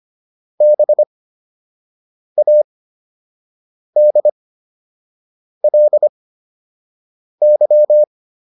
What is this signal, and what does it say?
Morse code, heard as a keyed tone, about 25 wpm, spaced as for 6 wpm, 600 Hz. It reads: BADLY